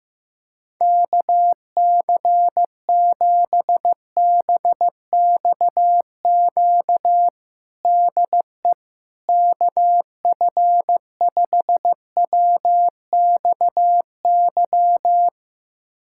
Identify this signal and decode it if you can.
Morse code, heard as a keyed tone, about 15 wpm, 700 Hz. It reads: KC7BXQ DE KF5WXY
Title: USA Callsign DE Callsign 15wpm